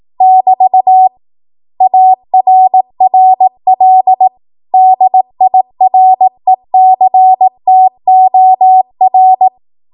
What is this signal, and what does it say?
Morse code, heard as a keyed tone, about 18 wpm, 750 Hz. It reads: <BT> ARRL DIRECTOR
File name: Archive / 18 WPM / 18WPM